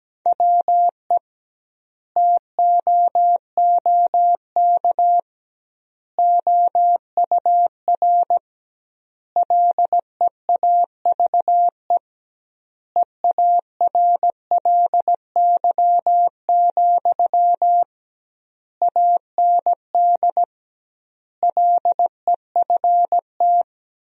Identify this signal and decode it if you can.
Morse code, heard as a keyed tone, about 17 wpm, 700 Hz. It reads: WE TOOK OUR LEAVE EARLY, AND LEFT
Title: 2x Word-Spacing - Sentences from Top 400 Words - 17wpm